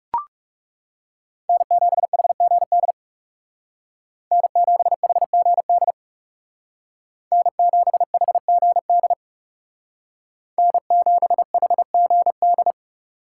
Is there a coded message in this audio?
Yes: Morse code from 1.5 to 12.7 s, heard as a keyed tone, about 40 wpm, 700 Hz. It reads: N7HGD N7HGD N7HGD N7HGD